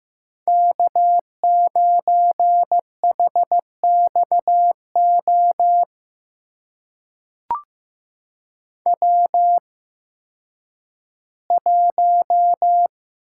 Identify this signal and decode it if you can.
Morse code, heard as a keyed tone, about 15 wpm, spaced as for 10 wpm, 700 Hz. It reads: K9HXO W1
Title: USA Call Signs - 5 Characters - 8x Character Spacing - 15wpm